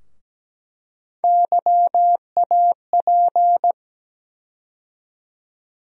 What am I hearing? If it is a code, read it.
Morse code, heard as a keyed tone, about 17 wpm, 700 Hz. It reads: YAP
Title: USA Call Sign Postfixes - 3 Letters 17wpm